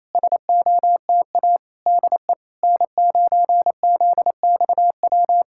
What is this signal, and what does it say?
Morse code, heard as a keyed tone, about 25 wpm, 700 Hz. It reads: SOTA DE N9ZXW
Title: SOTA QSO Practice 28wpm